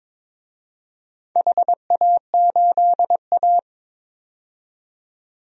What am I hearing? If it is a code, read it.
Morse code, heard as a keyed tone, about 21 wpm, 700 Hz. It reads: HA8A
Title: DXCC-100 Call Signs - 4 Characters - 22wpm